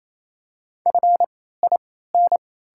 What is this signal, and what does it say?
Morse code, heard as a keyed tone, about 28 wpm, spaced as for 16 wpm, 700 Hz. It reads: FIN
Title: 3x Character Spacing - Top 100 Words - 28wpm